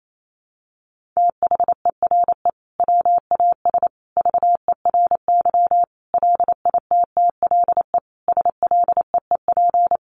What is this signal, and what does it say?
Morse code, heard as a keyed tone, about 28 wpm, 700 Hz. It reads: THERE WAS VERY LITTLE SLEEP
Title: Sentences from Top 1400 Words 28wpm